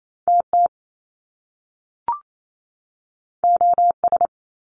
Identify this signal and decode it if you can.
Morse code, heard as a keyed tone, about 28 wpm, 700 Hz. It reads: TT OS
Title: Top Two Letter Combinations in Words 28wpm